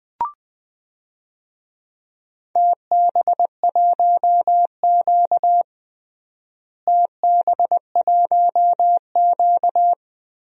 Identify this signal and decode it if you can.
Morse code, heard as a keyed tone, about 20 wpm, 700 Hz. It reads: TB1Q TB1Q
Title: DXCC-100 Call Signs - 4 Characters - Repeated 3x - 20wpm